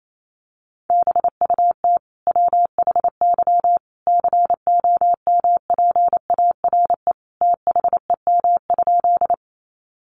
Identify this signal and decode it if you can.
Morse code, heard as a keyed tone, about 28 wpm, 700 Hz. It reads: BUT WHY COMPARE THEM?
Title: Sentences from Top 1300 Words 28wpm